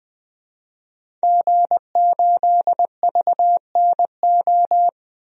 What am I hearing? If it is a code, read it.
Morse code, heard as a keyed tone, about 20 wpm, 700 Hz. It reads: G8VNO
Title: DXCC-100 Call Signs - 5 Characters - 20wpm